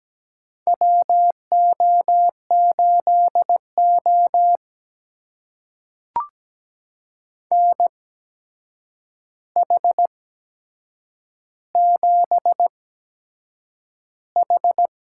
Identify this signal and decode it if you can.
Morse code, heard as a keyed tone, about 17 wpm, spaced as for 7 wpm, 700 Hz. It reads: WO8O NH7H
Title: USA Call Signs - 4 Characters - 8x Character Spacing - 17wpm